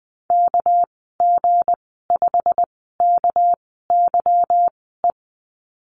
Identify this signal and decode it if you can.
Morse code, heard as a keyed tone, about 20 wpm, spaced as for 15 wpm, 700 Hz. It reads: KG5KYE
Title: 207 Lesson 33g - Letter j - Call Signs 20wpm